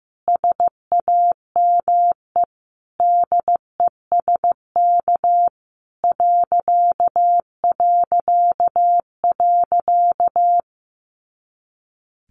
Morse code, heard as a keyed tone, about 15 wpm, 700 Hz. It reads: SAME DESK ...
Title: A Sound Of Thunder 15wpm